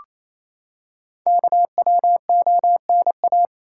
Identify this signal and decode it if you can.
Morse code, heard as a keyed tone, about 27 wpm, 700 Hz. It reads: KWONA
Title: Code Groups - 5 Letters - 28wpm